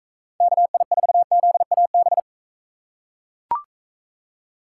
Morse code, heard as a keyed tone, about 41 wpm, 700 Hz. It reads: KI4ZAB